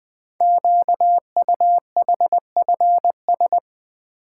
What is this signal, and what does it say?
Morse code, heard as a keyed tone, about 20 wpm, 700 Hz. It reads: QUHFS
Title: Code Groups - 5 Letters - 20wpm